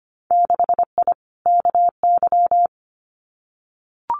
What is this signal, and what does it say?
Morse code, heard as a keyed tone, about 25 wpm, 700 Hz. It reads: 6I KY
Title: Winter Field Day Exchanges 25wpm